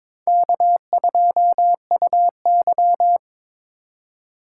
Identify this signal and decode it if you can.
Morse code, heard as a keyed tone, about 26 wpm, 700 Hz. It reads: K2UY
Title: Speed Racing - USA 1x2 Call Signs 22wpm